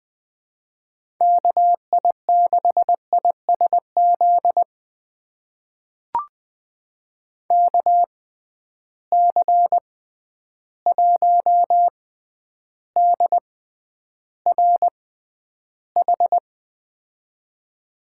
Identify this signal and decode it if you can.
Morse code, heard as a keyed tone, about 20 wpm, spaced as for 10 wpm, 700 Hz. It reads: KI6ISZ KC1DRH
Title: USA Call Signs - 6 Characters - 6x Character Spacing - 20wpm